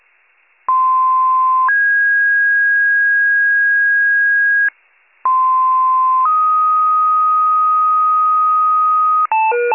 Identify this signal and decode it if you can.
Two-tone page, heard as a one-second tone, then a three-second tone, about 1040 then 1650 Hz, about 1040 then 1250 Hz, with noise. It is heard from 0.7 to 9.3 s.